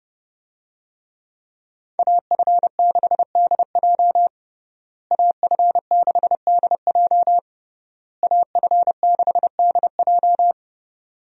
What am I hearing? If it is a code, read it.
Morse code, heard as a keyed tone, about 30 wpm, 700 Hz. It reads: AF6DJ AF6DJ AF6DJ
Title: USA Call Signs - 5 Characters - Repeated 3x - 30wpm